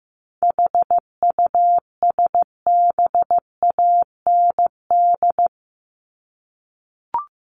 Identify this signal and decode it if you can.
Morse code, heard as a keyed tone, about 15 wpm, 700 Hz. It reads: HUSBAND
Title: Top 1501-1750 Words 15wpm